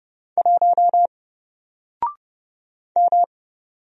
Morse code, heard as a keyed tone, about 30 wpm, 700 Hz. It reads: 1 M
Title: Single Character (Letter:Number:Punctuation:Prosign) 30wpm